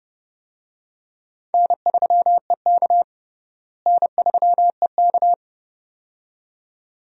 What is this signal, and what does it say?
Morse code, heard as a keyed tone, about 32 wpm, 700 Hz. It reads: N3EK N3EK
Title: USA Call Signs - 4 Characters - Repeated 2x - 30wpm